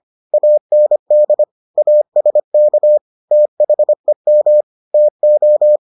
Morse code, heard as a keyed tone, about 25 wpm, 600 Hz. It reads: AND ASK THEM TO